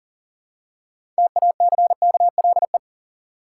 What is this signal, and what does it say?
Morse code, heard as a keyed tone, about 40 wpm, 700 Hz. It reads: TACKLE
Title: Six Letter Words - 40wpm